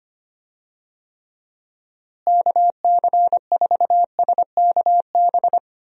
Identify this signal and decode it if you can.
Morse code, heard as a keyed tone, about 25 wpm, 700 Hz. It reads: KC4SKB
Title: USA Call Signs - 6 Characters - Repeated 2x - 25wpm